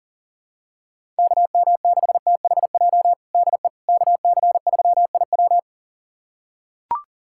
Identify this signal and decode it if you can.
Morse code, heard as a keyed tone, about 40 wpm, 700 Hz. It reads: KM6THJ DE KC3IW
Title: USA Callsign DE Callsign 40wpm